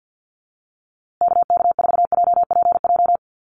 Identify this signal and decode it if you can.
Morse code, heard as a keyed tone, about 50 wpm, 700 Hz. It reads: KK4JPJ
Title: USA Call Signs Repeated 2x 50wpm